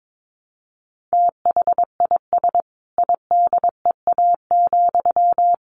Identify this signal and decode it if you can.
Morse code, heard as a keyed tone, about 22 wpm, 700 Hz. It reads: THIS IDEA,
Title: Sentences from Top 1300 Words 22wpm